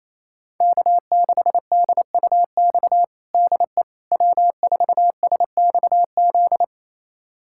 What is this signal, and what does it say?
Morse code, heard as a keyed tone, about 28 wpm, 700 Hz. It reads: K6DUX DE W4SXZ
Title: USA Callsign DE Callsign - 5 Characters - 28wpm